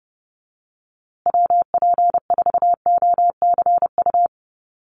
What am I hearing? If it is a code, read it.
Morse code, heard as a keyed tone, about 30 wpm, 700 Hz. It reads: WP4OCU
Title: USA Call Signs Repeated 3x 30wpm